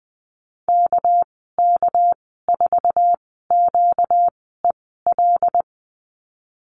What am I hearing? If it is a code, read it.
Morse code, heard as a keyed tone, about 20 wpm, spaced as for 15 wpm, 700 Hz. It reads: KK4QEL
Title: 245 Lesson 39g - Letter z - Call Signs 20wpm